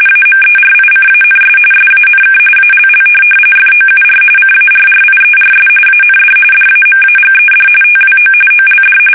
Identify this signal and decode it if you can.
RTTY, about 45.45 baud. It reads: AMERICANS SANK ALL FOUR CARRIERS AND DAMAGED TWO OLD
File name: fsk850 / USS-Massachusetts